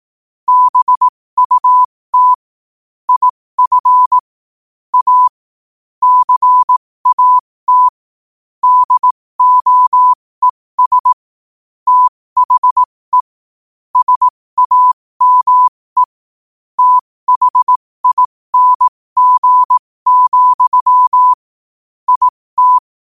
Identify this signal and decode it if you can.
Morse code, heard as a keyed tone, about 18 wpm, spaced as for 15 wpm, 1000 Hz. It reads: BUT IF A CAT DOES THE SAME THING, IT